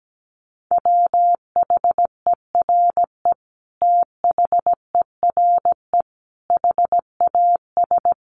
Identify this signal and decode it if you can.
Morse code, heard as a keyed tone, about 17 wpm, 700 Hz. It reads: WHERE THERE HAS
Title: Sentences from Top 1400 Words 17wpm